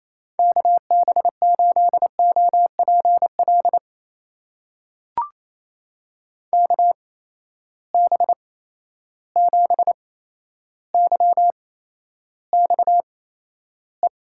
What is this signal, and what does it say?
Morse code, heard as a keyed tone, about 28 wpm, spaced as for 12 wpm, 700 Hz. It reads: KB8OPL KB7YXE
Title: USA Call Signs - 8x Character Spacing - 28wpm